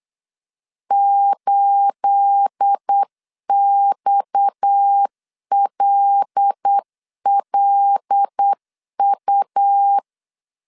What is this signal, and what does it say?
Morse code, heard as a keyed tone, about 8 wpm, 800 Hz. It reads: 8XLLU
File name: CW / DET / 1 9 1500C S A 8wpm 5637